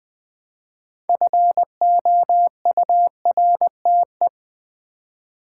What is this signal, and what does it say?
Morse code, heard as a keyed tone, about 20 wpm, 700 Hz. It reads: FOURTE
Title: Build A Word - 2 Letters At A Time - 8-Letter Words - 20wpm